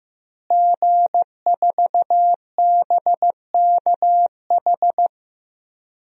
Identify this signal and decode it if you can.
Morse code, heard as a keyed tone, about 15 wpm, 700 Hz. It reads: G4BKH
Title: DXCC-100 Call Signs - 5 Characters - Repeated 3x - 15wpm